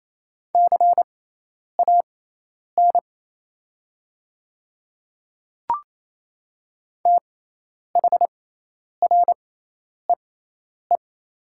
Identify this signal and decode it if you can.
Morse code, heard as a keyed tone, about 28 wpm, spaced as for 10 wpm, 700 Hz. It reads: CAN THREE
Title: 6x Character Spacing - Top 200 Words - 28wpm